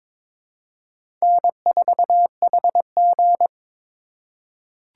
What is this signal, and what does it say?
Morse code, heard as a keyed tone, about 22 wpm, 700 Hz. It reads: N4HG
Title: USA Call Signs - 4 Characters - 3x Character Spacing - 22wpm